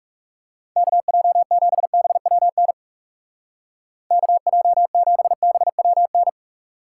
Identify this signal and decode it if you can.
Morse code, heard as a keyed tone, about 40 wpm, 700 Hz. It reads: KJ7BWN KJ7BWN